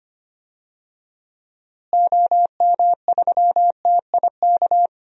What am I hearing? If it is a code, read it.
Morse code, heard as a keyed tone, about 25 wpm, 700 Hz. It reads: OM3TIK
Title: DXCC-100 Call Signs - 6 Characters - Repeated 3x - 25wpm